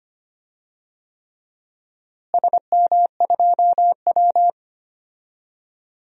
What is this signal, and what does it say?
Morse code, heard as a keyed tone, about 25 wpm, 700 Hz. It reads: SM2W
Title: DXCC-100 Call Signs - 4 Characters - Repeated 3x - 25wpm